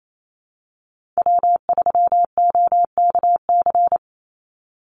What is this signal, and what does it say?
Morse code, heard as a keyed tone, about 28 wpm, 700 Hz. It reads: W3OKC
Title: Call Signs - 28wpm